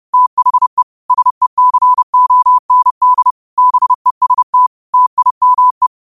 Morse code, heard as a keyed tone, about 30 wpm, 1000 Hz. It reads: THE SECOND BEST TIME
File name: qotdfiles / 30wpm